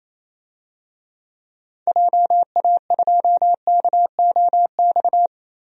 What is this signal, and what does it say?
Morse code, heard as a keyed tone, about 28 wpm, 700 Hz. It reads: JA2KOX
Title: DXCC-100 Call Signs - 6 Characters - 28wpm